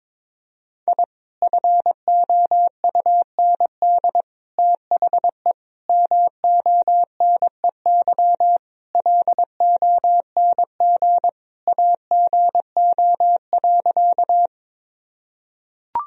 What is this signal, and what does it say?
Morse code, heard as a keyed tone, about 22 wpm, 700 Hz. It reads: I FOUND THE MONEY LONG AGO.
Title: Sentences from Top 1000 Words - Review Part 3 - 22wpm